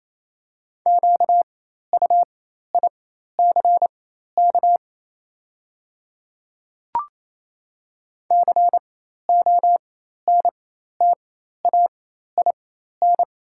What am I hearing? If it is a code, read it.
Morse code, heard as a keyed tone, about 28 wpm, spaced as for 13 wpm, 700 Hz. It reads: QUICK CONTAIN
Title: 4x Character Spacing - Top 500 Words - 28wpm